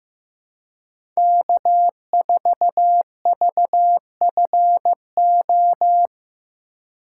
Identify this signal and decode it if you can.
Morse code, heard as a keyed tone, about 15 wpm, 700 Hz. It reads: K4VFO
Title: USA Call Signs - 5 Characters - 3x Character Spacing - 15wpm